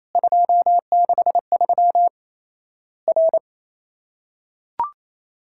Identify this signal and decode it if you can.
Morse code, heard as a keyed tone, about 28 wpm, 700 Hz. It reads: 263 R
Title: Worked All Europe DX Contest - QTC Exchange - 28wpm